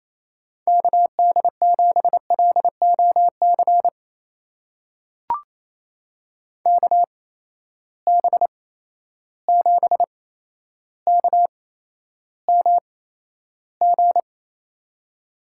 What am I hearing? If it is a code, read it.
Morse code, heard as a keyed tone, about 28 wpm, spaced as for 12 wpm, 700 Hz. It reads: KD7LOC KB7KMG
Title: USA Call Signs - 6 Characters - 8x Character Spacing - 28wpm